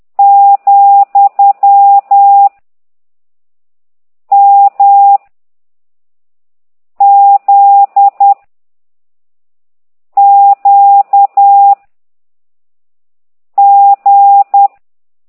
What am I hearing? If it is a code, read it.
Morse code, heard as a keyed tone, about 10 wpm, 800 Hz. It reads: , M Z Q G